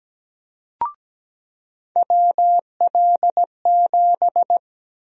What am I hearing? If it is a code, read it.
Morse code, heard as a keyed tone, about 17 wpm, 700 Hz. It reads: WL7
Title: USA Call Sign Prefixes - 17wpm